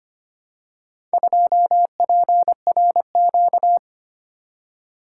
Morse code, heard as a keyed tone, about 26 wpm, 700 Hz. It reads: 2PRQ
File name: practice / 25wpm / Four Letter-Number Combinations - 25wpm